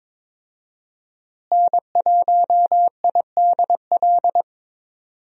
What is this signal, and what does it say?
Morse code, heard as a keyed tone, about 21 wpm, 700 Hz. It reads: N1IDL
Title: USA Call Signs - 5 Characters - Repeated 2x - 22wpm